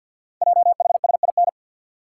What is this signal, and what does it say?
Morse code, heard as a keyed tone, about 47 wpm, 690 Hz. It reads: JHSIN
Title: Code Groups - 5 Letters - 50wpm